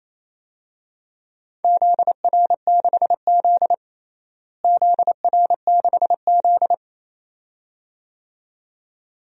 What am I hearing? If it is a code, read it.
Morse code, heard as a keyed tone, about 28 wpm, 700 Hz. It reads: ZR6Z ZR6Z
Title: DXCC-100 Call Signs - 4 Characters - Repeated 2x - 28wpm